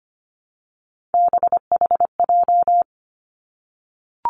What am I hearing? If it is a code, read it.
Morse code, heard as a keyed tone, about 25 wpm, 700 Hz. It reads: BHJ
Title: Three Challenging Letter-Number Combinations 25wpm